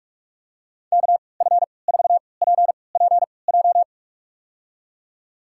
Morse code, heard as a keyed tone, about 46 wpm, spaced as for 26 wpm, 700 Hz. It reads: KF4PPJ
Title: USA Call Signs - 6 Characters - 3x Character Spacing - 45wpm